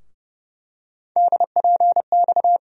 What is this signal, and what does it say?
Morse code, heard as a keyed tone, about 30 wpm, 700 Hz. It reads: DPX
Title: USA Call Sign Postfixes - 3 Letters 30wpm